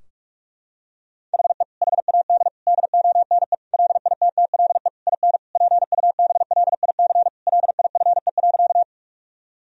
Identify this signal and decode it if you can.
Morse code, heard as a keyed tone, about 46 wpm, 700 Hz. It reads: HE HAD DONE LITTLE IN PUBLIC LIFE.
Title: Sentences from Top 1000 Words - Review Part 3 - 45wpm